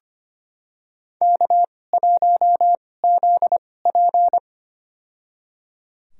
Morse code, heard as a keyed tone, about 25 wpm, spaced as for 18 wpm, 700 Hz. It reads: K1ZP
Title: USA Call Signs - 4 Characters - 2x Character Spacing - 25wpm